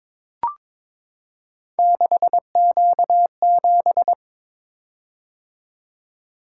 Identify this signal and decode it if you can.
Morse code, heard as a keyed tone, about 22 wpm, 700 Hz. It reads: 6Q7
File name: practice / 22wpm / Three Letter-Number Combinations - 22wpm